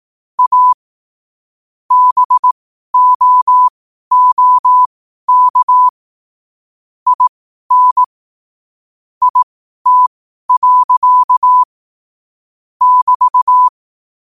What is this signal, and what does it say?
Morse code, heard as a keyed tone, about 18 wpm, spaced as for 12 wpm, 1000 Hz. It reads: A BOOK IN IT. <BT>
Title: 12 WPM morse code quotes for Thu, 29 Feb 2024